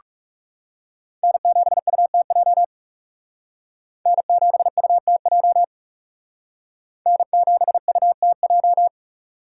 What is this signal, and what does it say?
Morse code, heard as a keyed tone, about 37 wpm, 700 Hz. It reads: N7UTJ N7UTJ N7UTJ
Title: Speed Racing - USA 1x3 Call Signs 30wpm